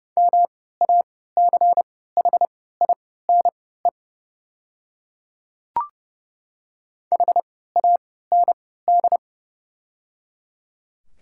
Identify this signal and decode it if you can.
Morse code, heard as a keyed tone, about 30 wpm, spaced as for 17 wpm, 700 Hz. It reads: MACHINE HAND
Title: 3x Character Spacing - Top 500 Words - 30wpm